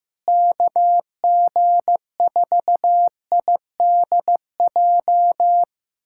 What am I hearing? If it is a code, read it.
Morse code, heard as a keyed tone, about 16 wpm, 700 Hz. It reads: KG4IDJ
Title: USA Callsign DE Callsign 15wpm